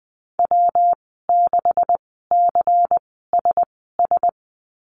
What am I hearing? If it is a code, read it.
Morse code, heard as a keyed tone, about 20 wpm, spaced as for 14 wpm, 700 Hz. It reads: W6CSS
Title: 140 Lesson 23g - Letter f - Call Signs 20wpm